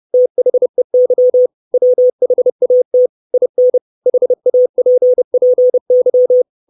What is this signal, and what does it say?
Morse code, heard as a keyed tone, about 30 wpm, 500 Hz. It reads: THEY WHAT IN HAPPY
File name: 632 Words 30WPM